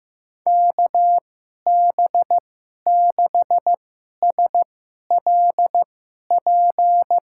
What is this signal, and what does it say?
Morse code, heard as a keyed tone, about 16 wpm, spaced as for 11 wpm, 700 Hz. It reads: KB6SLP
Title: USA Call Signs - 6 Characters - 2x Character Spacing - 15wpm